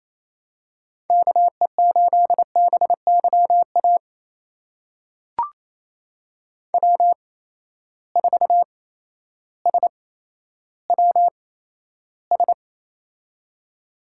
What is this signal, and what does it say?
Morse code, heard as a keyed tone, about 28 wpm, spaced as for 13 wpm, 700 Hz. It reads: KE8BYA W4SWS
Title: USA Call Signs - 8x Character Spacing - 28wpm